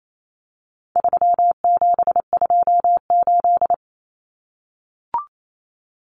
Morse code, heard as a keyed tone, about 28 wpm, 700 Hz. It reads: 3728
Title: Common Cut Numbers - 4-Digit 28wpm